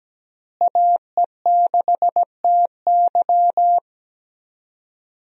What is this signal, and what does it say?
Morse code, heard as a keyed tone, about 17 wpm, 700 Hz. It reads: AE6TY
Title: USA Call Signs - 5 Characters - Repeated 2x - 17wpm